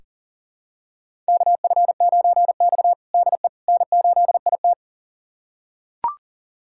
Morse code, heard as a keyed tone, about 40 wpm, 700 Hz. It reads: KF9X DE N8IT
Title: USA Callsign DE Callsign - 4 Characters - 40wpm